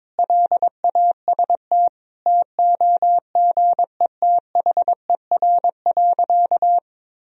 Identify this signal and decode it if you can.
Morse code, heard as a keyed tone, about 22 wpm, 700 Hz. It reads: LAST TOGETHER.
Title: Sentences from Top 1000 Words - Review Part 1 - 22wpm